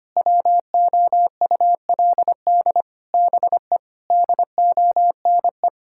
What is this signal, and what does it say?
Morse code, heard as a keyed tone, about 25 wpm, 700 Hz. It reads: WOULD BE DONE
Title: Sentences from Top 1000 Words - Review Part 5 - 25wpm